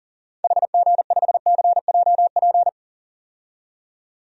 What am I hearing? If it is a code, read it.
Morse code, heard as a keyed tone, about 40 wpm, 700 Hz. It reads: HG5CJP
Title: DXCC-100 Call Signs - 6 Characters - 40wpm